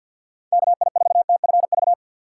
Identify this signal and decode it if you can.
Morse code, heard as a keyed tone, about 53 wpm, 700 Hz. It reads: KI4TFV
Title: USA Call Signs - 6 Characters - 3x Character Spacing - 50wpm